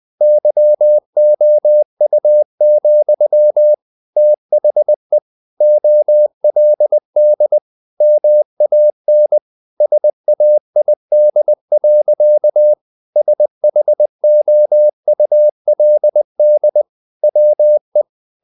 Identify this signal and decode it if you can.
Morse code, heard as a keyed tone, about 20 wpm, 600 Hz. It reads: YOU, THE OLD MAN SAID. SHOULD WE
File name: the old man and the sea / the old man and the sea part 1 20wpm